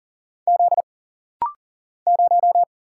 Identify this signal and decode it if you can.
Morse code, heard as a keyed tone, about 40 wpm, 700 Hz. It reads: Z 0
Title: Single Letter-Number Alt - Rapid-Fire 40wpm